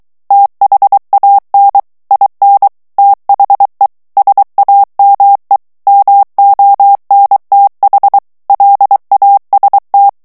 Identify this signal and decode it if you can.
Morse code, heard as a keyed tone, about 23 wpm, 800 Hz. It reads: THAN IN THE SAME MONTH LAST
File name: morse mp3 / ANGLAIS texte clair 2 vitesse 22 mots minute